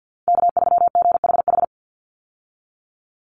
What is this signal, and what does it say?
Morse code, heard as a keyed tone, about 48 wpm, 700 Hz. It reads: K3GHH
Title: Call Signs - 50wpm